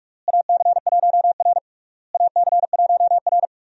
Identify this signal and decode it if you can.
Morse code, heard as a keyed tone, about 44 wpm, 700 Hz. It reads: AC1R AC1R
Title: USA Call Signs - 4 Characters - Repeated 3x - 45wpm